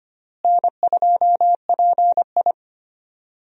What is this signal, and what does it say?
Morse code, heard as a keyed tone, about 25 wpm, 700 Hz. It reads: N2PI
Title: USA Call Signs - 4 Characters - Repeated 3x - 25wpm